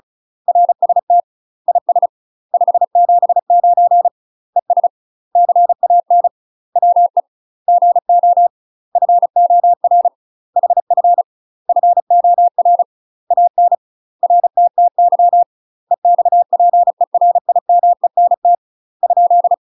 Morse code, heard as a keyed tone, about 36 wpm, 700 Hz. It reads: RST IS 579 ES CAN WE GO FOR HF FOR AN RTTY EXPERIMENT ?
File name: UK MASTER QSOs TRAINING / 35WPM / UK MASTER QSOs 0020